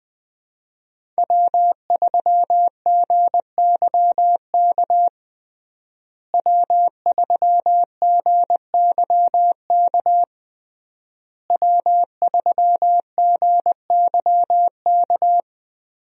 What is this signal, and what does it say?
Morse code, heard as a keyed tone, about 20 wpm, 700 Hz. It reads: W3GYK W3GYK W3GYK
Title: USA Call Signs - 5 Characters - Repeated 3x - 20wpm